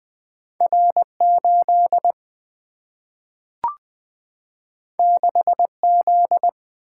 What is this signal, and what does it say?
Morse code, heard as a keyed tone, about 20 wpm, 700 Hz. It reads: R8 6Z
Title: Single-Letter Single-Number 20wpm